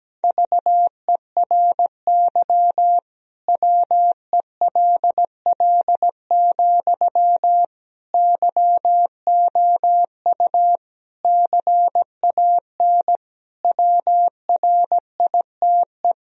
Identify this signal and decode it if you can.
Morse code, heard as a keyed tone, about 17 wpm, 700 Hz. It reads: VERY WELL, YOU CAN WRITE
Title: Sentences from Top 1000 Words - Review Part 4 - 17wpm